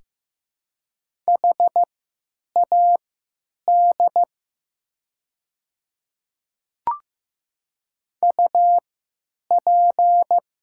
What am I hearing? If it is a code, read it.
Morse code, heard as a keyed tone, about 15 wpm, spaced as for 9 wpm, 700 Hz. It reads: HAD UP
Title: 3x Character Spacing - Top 100 Words - 15wpm